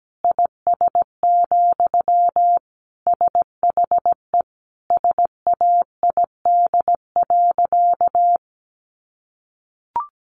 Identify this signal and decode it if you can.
Morse code, heard as a keyed tone, about 17 wpm, 700 Hz. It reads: IS, SHE SAID.
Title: Sentences from Top 100 Words 17wpm